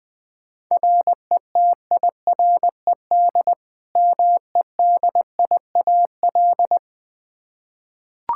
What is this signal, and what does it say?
Morse code, heard as a keyed tone, about 20 wpm, 700 Hz. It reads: RETIRED MEDIAL
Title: Practice QSO - Exchange 3 - More Info - 20wpm